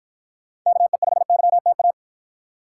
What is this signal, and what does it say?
Morse code, heard as a keyed tone, about 53 wpm, 700 Hz. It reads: KE5YTA